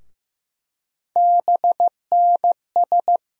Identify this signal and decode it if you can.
Morse code, heard as a keyed tone, about 15 wpm, 700 Hz. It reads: BNS
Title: USA Call Sign Postfixes - 3 Letters 15wpm